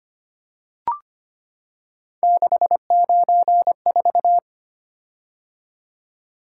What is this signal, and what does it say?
Morse code, heard as a keyed tone, about 25 wpm, 700 Hz. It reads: 694
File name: Three Letter-Number Combinations - 25wpm